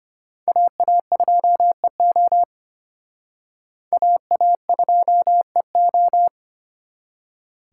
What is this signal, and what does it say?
Morse code, heard as a keyed tone, about 29 wpm, 700 Hz. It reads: AA2EO AA2EO
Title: Speed Racing - USA Call Signs 25wpm